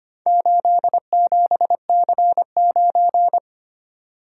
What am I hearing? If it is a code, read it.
Morse code, heard as a keyed tone, about 25 wpm, 700 Hz. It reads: 87C9
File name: practice / 25wpm / Four Letter-Number Combinations - 25wpm